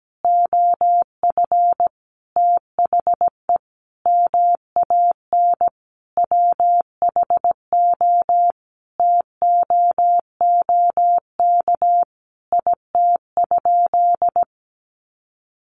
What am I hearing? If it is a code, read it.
Morse code, heard as a keyed tone, about 17 wpm, 700 Hz. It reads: OF THE MAN WHO TOOK IT?
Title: Sentences from Top 300 Words 17wpm